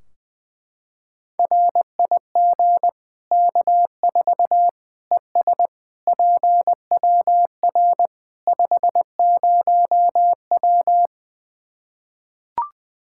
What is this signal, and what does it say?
Morse code, heard as a keyed tone, about 20 wpm, 700 Hz. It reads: RIG K4 ES PWR 50W
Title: Practice QSO - Exchange 2 - Info - 20wpm